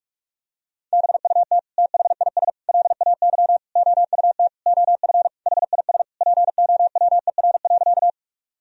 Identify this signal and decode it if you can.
Morse code, heard as a keyed tone, about 46 wpm, 700 Hz. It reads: BUT THIS LAY OUT OF HIS POWER.
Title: Sentences from Top 1000 Words - Review Part 2 - 45wpm